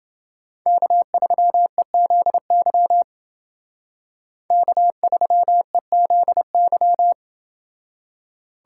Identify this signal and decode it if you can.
Morse code, heard as a keyed tone, about 28 wpm, 700 Hz. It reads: K3EZY K3EZY